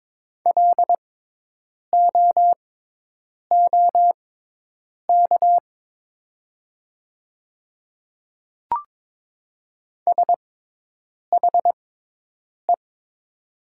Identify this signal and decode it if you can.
Morse code, heard as a keyed tone, about 22 wpm, spaced as for 8 wpm, 700 Hz. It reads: LOOK SHE
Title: 6x Character Spacing - Top 100 Words - 22wpm